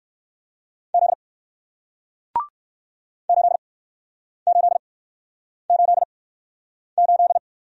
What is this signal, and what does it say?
Morse code, heard as a keyed tone, about 61 wpm, spaced as for 9 wpm, 700 Hz. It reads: Z 8888